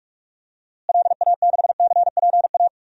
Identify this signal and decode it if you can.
Morse code, heard as a keyed tone, about 46 wpm, 700 Hz. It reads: LA6CPA
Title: DXCC-100 Call Signs - 6 Characters - Repeated 2x - 45wpm